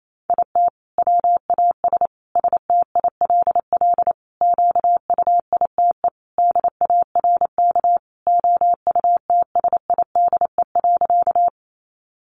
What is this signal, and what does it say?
Morse code, heard as a keyed tone, about 28 wpm, 700 Hz. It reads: IT WAS STILL QUITE DARK OUTSIDE.
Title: Sentences from Top 900 Words 28wpm